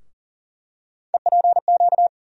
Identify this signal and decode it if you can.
Morse code, heard as a keyed tone, about 40 wpm, 700 Hz. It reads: EPQ
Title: USA Call Sign Postfixes - 3 Letters 40wpm